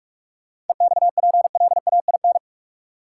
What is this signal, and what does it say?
Morse code, heard as a keyed tone, about 47 wpm, 700 Hz. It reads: EXPLAIN
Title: Sentences from Top 1100 Words - New Word Drill Alt - 45wpm